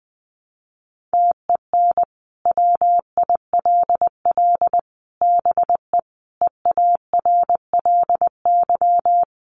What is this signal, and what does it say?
Morse code, heard as a keyed tone, about 19 wpm, 700 Hz. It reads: TEN WILL BE EARLY
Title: Sentences from Top 400 Words 20wpm